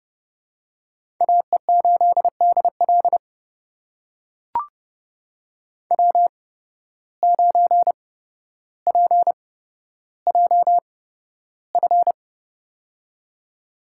Morse code, heard as a keyed tone, about 30 wpm, spaced as for 13 wpm, 700 Hz. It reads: AE8DL W9PJF
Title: USA Call Signs - 5 Characters - 8x Character Spacing - 30wpm